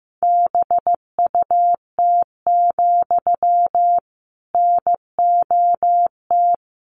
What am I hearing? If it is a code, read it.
Morse code, heard as a keyed tone, about 15 wpm, 700 Hz. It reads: BUT, NOT
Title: Sentences from Top 600 Words 15wpm